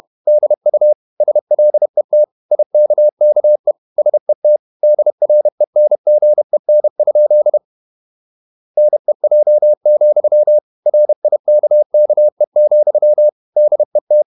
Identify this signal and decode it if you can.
Morse code, heard as a keyed tone, about 31 wpm, 600 Hz. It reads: DU SLET IKKE SET DRENGEN? NEJ, RIKKE, DET
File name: Ferie QRQ 31wpm